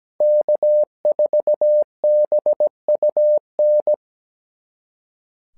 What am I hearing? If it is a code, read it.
Morse code, heard as a keyed tone, about 17 wpm, 600 Hz. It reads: K4BUN
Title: SOTA QSO Practice 17wpm